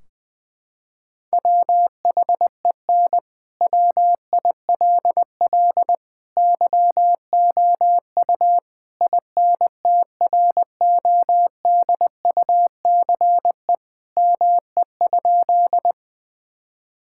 Morse code, heard as a keyed tone, about 20 wpm, 700 Hz. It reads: WHEN WILL YOU INTRODUCE ME?
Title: Sentences from Top 1600 Words 20wpm